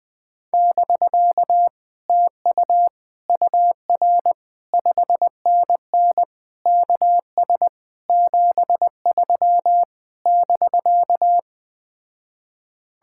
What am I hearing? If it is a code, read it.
Morse code, heard as a keyed tone, about 20 wpm, 700 Hz. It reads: <BK> TU UR 5NN KS 73 <BK>
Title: Speed-Racing Common QSO Phrases - 17wpm